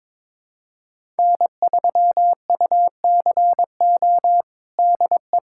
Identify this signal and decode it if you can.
Morse code, heard as a keyed tone, about 22 wpm, 700 Hz. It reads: N3UCO DE
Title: USA Callsign DE Callsign - 5 Characters - 22wpm